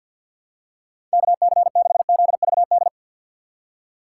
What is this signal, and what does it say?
Morse code, heard as a keyed tone, about 49 wpm, 700 Hz. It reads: KC6ZVD
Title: USA Call Signs - 6 Characters - 3x Character Spacing - 50wpm